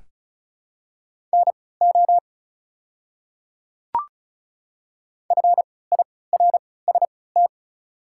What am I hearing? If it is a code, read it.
Morse code, heard as a keyed tone, about 36 wpm, spaced as for 20 wpm, 700 Hz. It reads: NO FIRST
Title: 3x Character Spacing - Top 200 Words - 35wpm